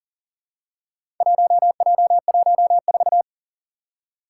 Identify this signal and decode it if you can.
Morse code, heard as a keyed tone, about 38 wpm, 700 Hz. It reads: 1J14
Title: Four Challenging Letter-Number Combinations 40wpm